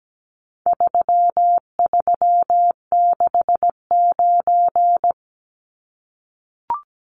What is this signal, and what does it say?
Morse code, heard as a keyed tone, about 17 wpm, 700 Hz. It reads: 3369
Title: Numbers - 4-Digit 17wpm